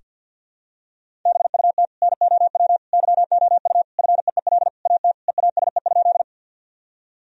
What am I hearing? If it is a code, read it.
Morse code, heard as a keyed tone, about 47 wpm, 700 Hz. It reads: BUT NOW YOU FEEL AT EASE?
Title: Sentences from Top 1000 Words - Review Part 1 - 50wpm